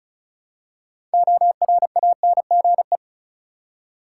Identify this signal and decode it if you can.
Morse code, heard as a keyed tone, about 34 wpm, 700 Hz. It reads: ORANGE
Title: Six Letter Words - 35wpm